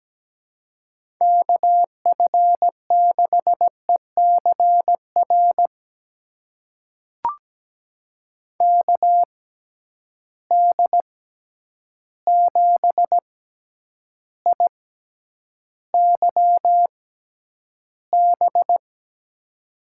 Morse code, heard as a keyed tone, about 17 wpm, spaced as for 9 wpm, 700 Hz. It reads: KF6ECR KD7IYB
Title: USA Call Signs - 6 Characters - 6x Character Spacing - 17wpm